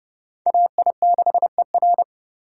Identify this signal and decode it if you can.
Morse code, heard as a keyed tone, about 28 wpm, 700 Hz. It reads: AI6ER